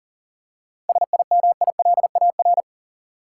Morse code, heard as a keyed tone, about 40 wpm, 700 Hz. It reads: SIMILAR
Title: Seven Letter Words - 40wpm